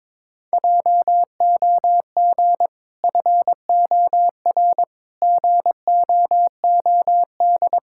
Morse code, heard as a keyed tone, about 23 wpm, 700 Hz. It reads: JOG FOR GOOD
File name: practice / 22wpm / Easy Sentences with Hints - 22wpm